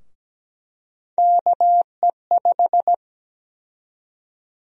Morse code, heard as a keyed tone, about 17 wpm, 700 Hz. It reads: KE5
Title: USA Call Sign Prefixes - Repeated 2x - 17wpm